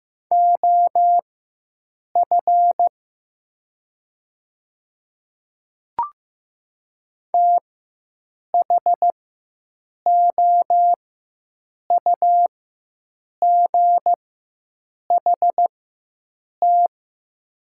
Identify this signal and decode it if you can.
Morse code, heard as a keyed tone, about 15 wpm, spaced as for 7 wpm, 700 Hz. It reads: OF THOUGHT
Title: 4x Character Spacing - Top 300 Words - 15wpm